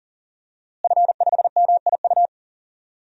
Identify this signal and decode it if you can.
Morse code, heard as a keyed tone, about 40 wpm, 700 Hz. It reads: F5MIU
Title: DXCC-100 Call Signs - 5 Characters - Repeated 2x - 40wpm